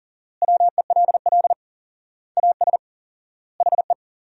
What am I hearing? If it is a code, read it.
Morse code, heard as a keyed tone, about 37 wpm, 700 Hz. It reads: WELL AS HE
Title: 4x Word-Spacing - Sentences from Top 500 Words - 40wpm